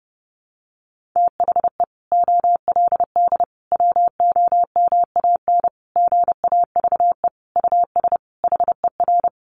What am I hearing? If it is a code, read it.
Morse code, heard as a keyed tone, about 30 wpm, 700 Hz. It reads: THE OLD WOMAN GAVE US HER
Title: Sentences from Top 1100 Words 30wpm